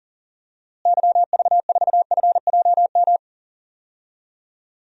Morse code, heard as a keyed tone, about 40 wpm, 700 Hz. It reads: YV4FJM
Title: DXCC-100 Call Signs - 6 Characters - Repeated 2x - 40wpm